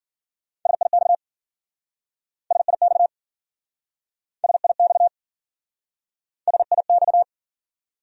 Morse code, heard as a keyed tone, about 52 wpm, 700 Hz. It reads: SIX SIX SIX SIX